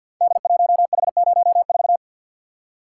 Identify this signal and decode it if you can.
Morse code, heard as a keyed tone, about 51 wpm, 700 Hz. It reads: D1H04
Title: Code Groups - 5 Letter:Numbers - 50wpm